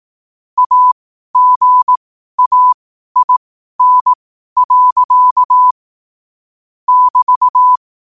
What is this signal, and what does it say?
Morse code, heard as a keyed tone, about 18 wpm, spaced as for 12 wpm, 1000 Hz. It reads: AGAIN. <BT>
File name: qotdfiles / 12wpm